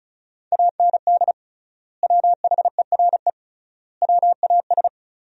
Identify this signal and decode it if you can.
Morse code, heard as a keyed tone, about 34 wpm, 700 Hz. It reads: AND WHERE WAS
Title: 3x Word-Spacing - Sentences from Top 200 Words - 35wpm